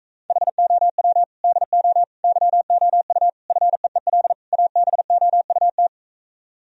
Morse code, heard as a keyed tone, about 44 wpm, 700 Hz. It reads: HOW DO YOU FEEL ABOUT